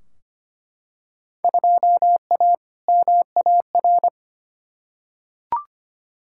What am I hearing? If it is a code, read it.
Morse code, heard as a keyed tone, about 25 wpm, 700 Hz. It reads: 2A MAR
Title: Field Day Exchanges - 25wpm